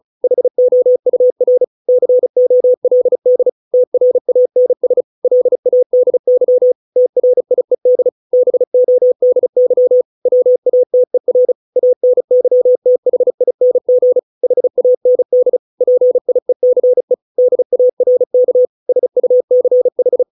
Morse code, heard as a keyed tone, about 34 wpm, 500 Hz. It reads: HOUR COLD TRANS LADY TRIED BODY WATER ANYTHING HAND PIECE DARK SUCH